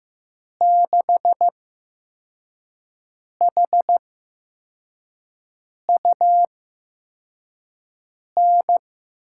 Morse code, heard as a keyed tone, about 15 wpm, spaced as for 4 wpm, 700 Hz. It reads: 6HUN
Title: USA Call Signs - 8x Character Spacing - 15wpm